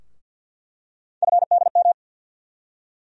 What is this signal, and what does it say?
Morse code, heard as a keyed tone, about 48 wpm, 700 Hz. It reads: FDM
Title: USA Call Sign Postfixes - 3 Letters 50wpm